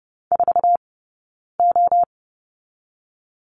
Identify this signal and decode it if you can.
Morse code, heard as a keyed tone, about 30 wpm, 700 Hz. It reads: 4 O
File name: course / 30:10wpm / 086 Lesson 15d - Number 5 - Two Characters 30wpm